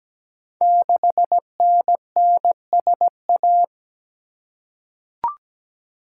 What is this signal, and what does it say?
Morse code, heard as a keyed tone, about 17 wpm, 690 Hz. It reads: 6NNSA
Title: Code Groups - 5 Letter:Numbers - 17wpm